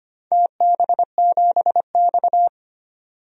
Morse code, heard as a keyed tone, about 25 wpm, 700 Hz. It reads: TB7X
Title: DXCC-100 Call Signs - 4 Characters - 25wpm